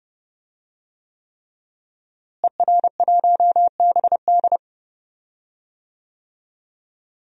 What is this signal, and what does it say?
Morse code, heard as a keyed tone, about 31 wpm, 700 Hz. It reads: ER1BD
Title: DXCC-100 Call Signs - 5 Characters - Repeated 3x - 30wpm